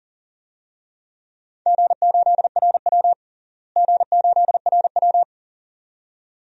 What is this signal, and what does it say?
Morse code, heard as a keyed tone, about 37 wpm, 700 Hz. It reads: G8RW G8RW
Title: DXCC-100 Call Signs - 4 Characters - Repeated 2x - 40wpm